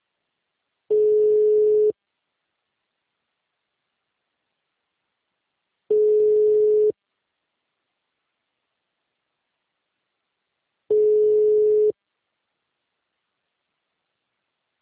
Ringback tone (Europe, 425 Hz).
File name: out- -2044- - - .51